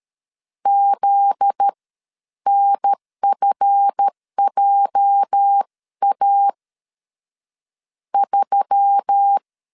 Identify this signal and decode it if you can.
Morse code, heard as a keyed tone, about 13 wpm, 800 Hz. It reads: Z NFJA 3
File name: LEZIONI / 8 YJ45 / 8 YJ 12wpm 2235